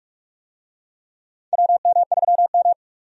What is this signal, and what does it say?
Morse code, heard as a keyed tone, about 43 wpm, 700 Hz. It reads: WM3M
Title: USA Call Signs - 4 Characters - 45wpm